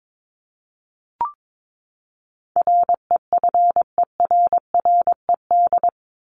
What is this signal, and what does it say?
Morse code, heard as a keyed tone, about 22 wpm, 700 Hz. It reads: REFERRED
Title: ED Suffix - Top 100 Words with Base Words 22wpm